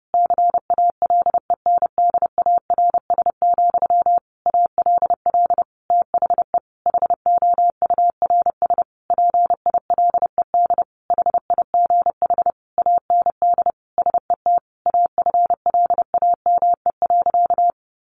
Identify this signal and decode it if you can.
Morse code, heard as a keyed tone, about 30 wpm, 700 Hz. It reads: CALENDARS, ALL THE HOURS PILED HIGH AND SET AFLAME.
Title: A Sound Of Thunder 30wpm